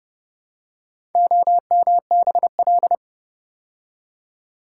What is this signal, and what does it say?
Morse code, heard as a keyed tone, about 30 wpm, 700 Hz. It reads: OMBL